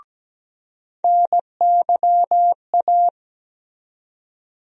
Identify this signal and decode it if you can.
Morse code, heard as a keyed tone, about 17 wpm, 700 Hz. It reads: NYA
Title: USA Call Sign Postfixes - 3 Letters 17wpm